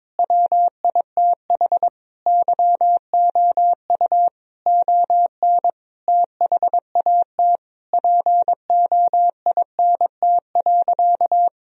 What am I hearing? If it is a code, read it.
Morse code, heard as a keyed tone, about 23 wpm, 700 Hz. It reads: WITH YOU ON THAT POINT.
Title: Sentences from Top 1000 Words - Review Part 1 - 22wpm